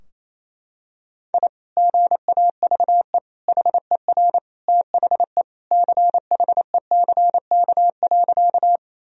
Morse code, heard as a keyed tone, about 27 wpm, 700 Hz. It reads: I GAVE HER THE CHECK.
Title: Sentences from Top 1000 Words - Review Part 3 - 28wpm